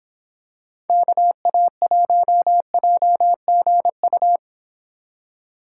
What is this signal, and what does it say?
Morse code, heard as a keyed tone, about 26 wpm, 700 Hz. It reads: KA1JGU